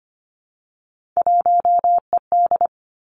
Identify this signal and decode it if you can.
Morse code, heard as a keyed tone, about 25 wpm, 700 Hz. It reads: 1ED
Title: Three Challenging Letter-Number Combinations 25wpm